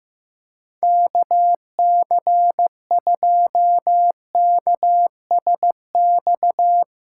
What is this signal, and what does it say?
Morse code, heard as a keyed tone, about 15 wpm, 700 Hz. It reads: KC2KSX
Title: USA Call Signs - 6 Characters - 8x Character Spacing - 15wpm